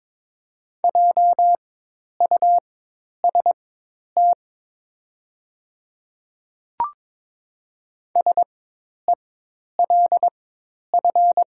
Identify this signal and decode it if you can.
Morse code, heard as a keyed tone, about 21 wpm, spaced as for 10 wpm, 700 Hz. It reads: JUST SELF
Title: 4x Character Spacing - Top 200 Words - 22wpm